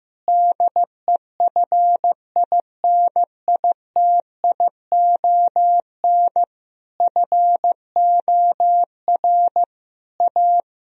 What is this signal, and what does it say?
Morse code, heard as a keyed tone, about 15 wpm, 700 Hz. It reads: DEFINITION FOR A
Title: Sentences from Top 1000 Words - Review Part 4 - 15wpm